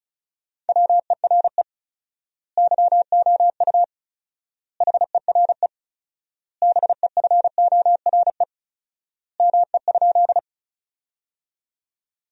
Morse code, heard as a keyed tone, about 34 wpm, 700 Hz. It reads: WERE YOU HERE BEFORE ME?
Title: 4x Word-Spacing - Sentences from Top 200 Words - 35wpm